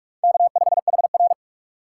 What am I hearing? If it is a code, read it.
Morse code, heard as a keyed tone, about 44 wpm, 700 Hz. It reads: K5HR